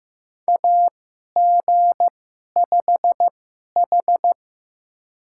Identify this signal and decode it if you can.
Morse code, heard as a keyed tone, about 15 wpm, spaced as for 11 wpm, 700 Hz. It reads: AG5H
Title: USA Call Signs - 4 Characters - 2x Character Spacing - 15wpm